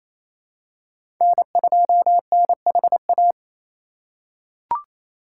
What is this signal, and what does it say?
Morse code, heard as a keyed tone, about 28 wpm, 700 Hz. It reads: N2NHA
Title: USA Call Signs - 3x Character Spacing - 28wpm